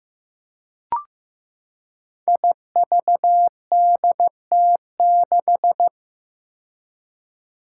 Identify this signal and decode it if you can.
Morse code, heard as a keyed tone, about 15 wpm, 700 Hz. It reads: IVDT6
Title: Five Challenging Letter-Number Combinations 15wpm